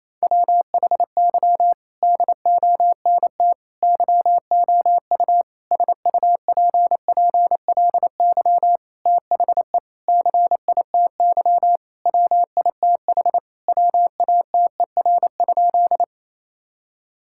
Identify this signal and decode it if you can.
Morse code, heard as a keyed tone, about 28 wpm, 700 Hz. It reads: WHY DONT YOU SUPPLY THE CITY WITH WATER?
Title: Sentences from Top 1000 Words - Review Part 5 - 28wpm